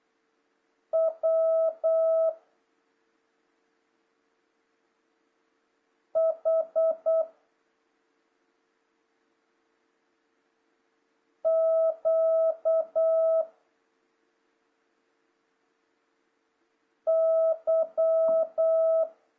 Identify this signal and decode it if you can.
Morse code, heard as a keyed tone, about 8 wpm, 650 Hz. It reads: W H Q Y
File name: jh8cbh / 2026cw